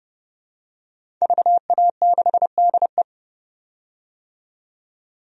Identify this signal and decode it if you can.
Morse code, heard as a keyed tone, about 30 wpm, 700 Hz. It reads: VA6DE
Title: DXCC-100 Call Signs - 5 Characters - 30wpm